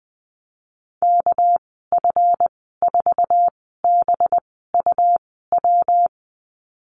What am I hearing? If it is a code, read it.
Morse code, heard as a keyed tone, about 20 wpm, spaced as for 15 wpm, 700 Hz. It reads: KF4BUW
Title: 219 Lesson 35g - Number 0 - Call Signs 20wpm